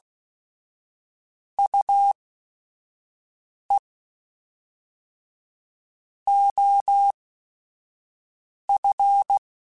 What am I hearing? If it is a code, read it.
Morse code, heard as a keyed tone, about 16 wpm, spaced as for 5 wpm, 800 Hz. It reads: UE OF
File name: Technical Subjects / Technical Subjects-mp3s / XL @5